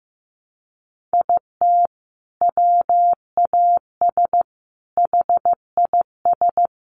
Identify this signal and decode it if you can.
Morse code, heard as a keyed tone, about 15 wpm, 700 Hz. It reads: IT WAS HIS
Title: Sentences from Top 1500 Words 15wpm